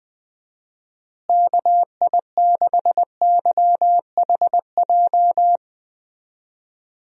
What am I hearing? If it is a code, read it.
Morse code, heard as a keyed tone, about 20 wpm, 700 Hz. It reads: KI6YHJ
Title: USA Call Signs - 6 Characters - Repeated 2x - 20wpm